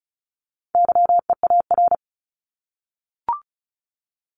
Morse code, heard as a keyed tone, about 34 wpm, 700 Hz. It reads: YEAR
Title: ES:S Suffix - Top 100 Words with Base Words 35wpm